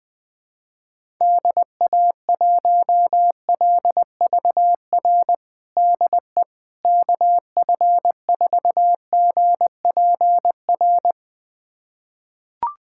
Morse code, heard as a keyed tone, about 21 wpm, 700 Hz. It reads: DA1LVR DE KF4GPR
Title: USA Callsign DE Callsign - 6 Characters - 20wpm